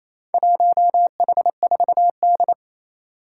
Morse code, heard as a keyed tone, about 29 wpm, 700 Hz. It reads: 1H4D
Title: Four Challenging Letter-Number Combinations 28wpm